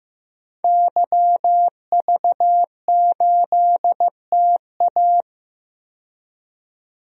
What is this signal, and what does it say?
Morse code, heard as a keyed tone, about 15 wpm, 700 Hz. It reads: YV8TA
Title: DXCC-100 Call Signs - 5 Characters - 15wpm